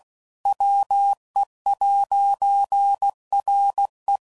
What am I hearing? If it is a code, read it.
Morse code, heard as a keyed tone, about 16 wpm, 800 Hz. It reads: WE'RE
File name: CW Code Groups @15WPM / Technical Subjects / AND Gate